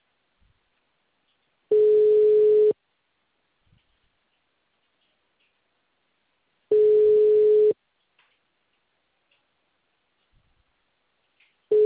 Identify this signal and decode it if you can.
Ringback tone (Europe, 425 Hz).
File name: out- -2012- - - .10021